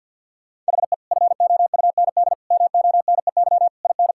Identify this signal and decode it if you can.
Morse code, heard as a keyed tone, about 47 wpm, 700 Hz. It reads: HE FOUND MONEY IN
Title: Sentences from Top 1000 Words - Review Part 6 - 50wpm